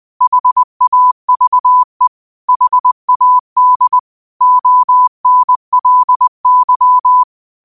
Morse code, heard as a keyed tone, about 20 wpm, 1000 Hz. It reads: HAVE HAD ONLY